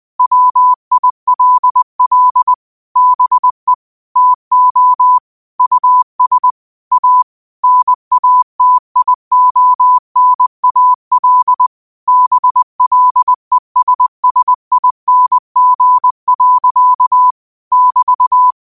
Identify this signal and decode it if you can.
Morse code, heard as a keyed tone, about 20 wpm, 1000 Hz. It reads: WILL BE TO US A NATIONAL BLESSING. <BT>
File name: qotdfiles / 20wpm